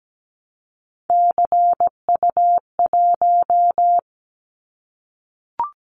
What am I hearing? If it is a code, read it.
Morse code, heard as a keyed tone, about 17 wpm, 700 Hz. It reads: CU1
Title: Callsign Prefixes for Top 50 DXCC - 17wpm